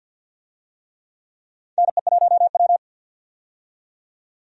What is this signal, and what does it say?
Morse code, heard as a keyed tone, about 51 wpm, 700 Hz. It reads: NE1W
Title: USA Call Signs - 4 Characters - 50wpm